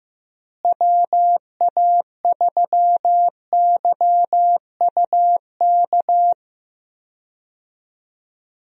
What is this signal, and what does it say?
Morse code, heard as a keyed tone, about 15 wpm, 700 Hz. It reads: WA3YUK
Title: USA Call Signs - 6 Characters - Repeated 2x - 15wpm